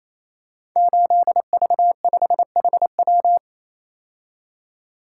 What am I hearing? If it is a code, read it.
Morse code, heard as a keyed tone, about 27 wpm, 700 Hz. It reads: 8V5HW
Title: Code Groups - 5 Letter:Numbers - 28wpm